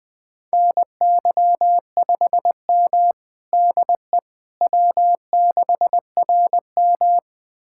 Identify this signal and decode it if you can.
Morse code, heard as a keyed tone, about 21 wpm, 700 Hz. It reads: NY5M DE W6RM
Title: USA Callsign DE Callsign - 4 Characters - 20wpm